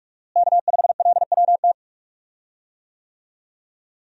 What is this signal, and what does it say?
Morse code, heard as a keyed tone, about 46 wpm, 700 Hz. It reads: K5LWT
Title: USA Call Signs - 5 Characters - Repeated 3x - 45wpm